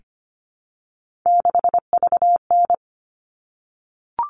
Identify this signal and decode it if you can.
Morse code, heard as a keyed tone, about 25 wpm, 700 Hz. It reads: 6VN
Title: All Cut Numbers - 3-Digit 25wpm